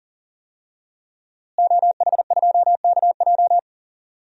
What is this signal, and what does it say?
Morse code, heard as a keyed tone, about 37 wpm, 700 Hz. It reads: OH2KJ
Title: DXCC-100 Call Signs - 5 Characters - Repeated 2x - 40wpm